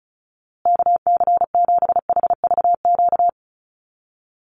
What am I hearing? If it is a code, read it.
Morse code, heard as a keyed tone, about 34 wpm, 700 Hz. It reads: KC7HVQ
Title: USA Call Signs Repeated 3x 35wpm